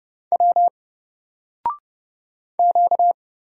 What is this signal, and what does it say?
Morse code, heard as a keyed tone, about 30 wpm, 700 Hz. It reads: W Q
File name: Single-Letter-Number-30wpm